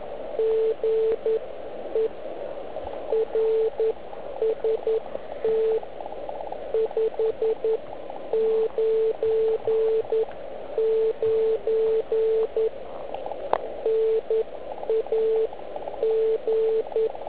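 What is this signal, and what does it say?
Morse code, heard as a keyed tone, about 11 wpm, spaced as for 9 wpm, 440 Hz. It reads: GE RST 599 NAG
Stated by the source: Slow (QRS) CW test by beginner OK1IFW during beginner Morse skeds on 3559.5 kHz (80 m).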